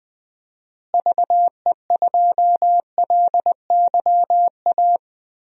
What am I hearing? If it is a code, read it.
Morse code, heard as a keyed tone, about 20 wpm, 700 Hz. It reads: VE2LYA
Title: DXCC-100 Call Signs - 6 Characters - Repeated 2x - 20wpm